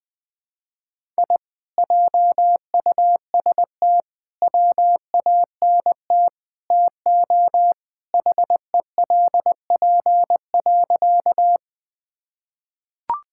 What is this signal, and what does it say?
Morse code, heard as a keyed tone, about 20 wpm, 700 Hz. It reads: I JUST WANT TO HELP.
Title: Sentences with 4-Word Phrases - 20wpm